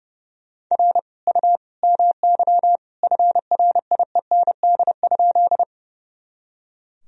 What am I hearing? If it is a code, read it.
Morse code, heard as a keyed tone, about 30 wpm, 700 Hz. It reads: R U MY FRIEND?
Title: Sentences from Top 300 Words and Abbreviations 30wpm